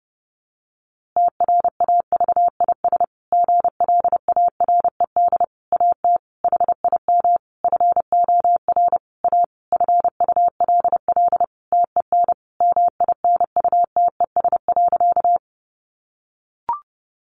Morse code, heard as a keyed tone, about 30 wpm, 700 Hz. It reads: TRAVIS GLARED AT HIM FOR A FULL TEN MINUTES.
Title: A Sound Of Thunder 30wpm